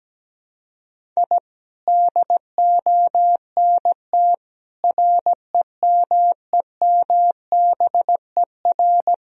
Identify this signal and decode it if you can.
Morse code, heard as a keyed tone, about 17 wpm, 700 Hz. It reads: I DONT REMEMBER
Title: Sentences from Top 1000 Words - Review Part 1 - 17wpm